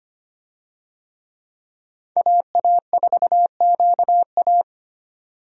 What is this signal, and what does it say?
Morse code, heard as a keyed tone, about 25 wpm, 700 Hz. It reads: AA4QA
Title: USA Call Signs - 5 Characters - 25wpm